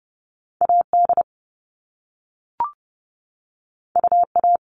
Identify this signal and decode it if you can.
Morse code, heard as a keyed tone, about 30 wpm, 700 Hz. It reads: A D U A
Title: All Cut Numbers - 2-Digit 30wpm